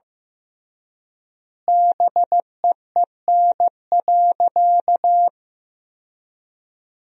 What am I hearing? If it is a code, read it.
Morse code, heard as a keyed tone, about 16 wpm, 700 Hz. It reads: BEEN.
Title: 3x Word-Spacing - Sentences from Top 300 Words - 15wpm